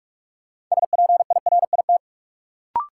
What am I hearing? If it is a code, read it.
Morse code, heard as a keyed tone, about 46 wpm, 700 Hz. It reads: SPIRIT
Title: Sentences from Top 1400 Words - New Word Drill Alt - 45wpm